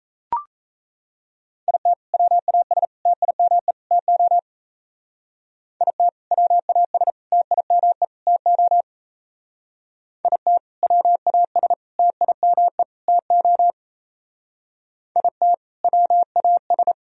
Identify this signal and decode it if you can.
Morse code, heard as a keyed tone, about 34 wpm, 700 Hz. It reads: IT WAS TIME TO IT WAS TIME TO IT WAS TIME TO IT WAS
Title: Speed-Racing - Sets of 4 Words 28wpm